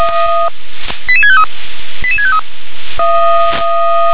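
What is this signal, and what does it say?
Five-tone selective call (ZVEI), heard as quick runs of five tones, tones of about 70 ms. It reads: 88653 88653
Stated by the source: ZVEI five-tone call received with an SDR.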